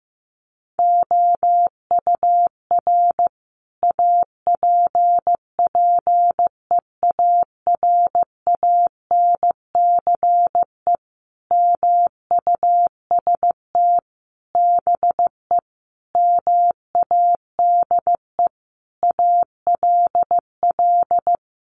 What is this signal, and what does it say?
Morse code, heard as a keyed tone, about 15 wpm, 700 Hz. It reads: OUR APPEARANCE MUST BE MADE ALL
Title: Sentences from Top 1100 Words 15wpm